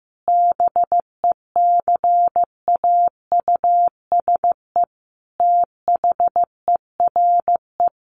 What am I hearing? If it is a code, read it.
Morse code, heard as a keyed tone, about 15 wpm, 700 Hz. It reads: BECAUSE THERE
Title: Sentences from Top 1000 Words 15wpm